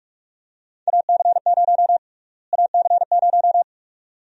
Morse code, heard as a keyed tone, about 41 wpm, 700 Hz. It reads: AC0 AC0
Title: USA Call Sign Prefixes - Repeated 2x - 45wpm